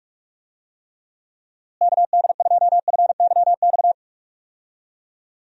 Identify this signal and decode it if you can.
Morse code, heard as a keyed tone, about 43 wpm, 700 Hz. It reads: KD2FYX
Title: USA Call Signs - 6 Characters - Repeated 2x - 45wpm